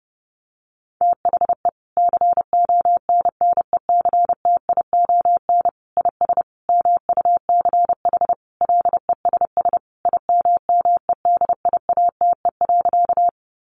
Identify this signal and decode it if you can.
Morse code, heard as a keyed tone, about 30 wpm, 700 Hz. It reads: THE CONNECTION IS MUCH LESS IMMEDIATE.
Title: Sentences from Top 1300 Words 30wpm